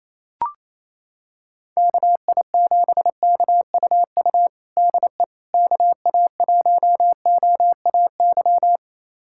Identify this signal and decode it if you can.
Morse code, heard as a keyed tone, about 28 wpm, 700 Hz. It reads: KI7KUU DE KA1OAY
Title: USA Callsign DE Callsign 28wpm